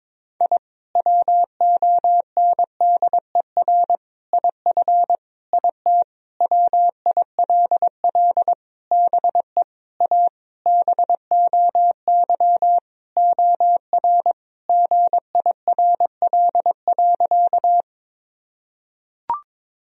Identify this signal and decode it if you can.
Morse code, heard as a keyed tone, about 22 wpm, 700 Hz. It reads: I WONDER IF IT WILL BE A BOY OR GIRL.
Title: Sentences from Top 1000 Words - Review Part 3 - 22wpm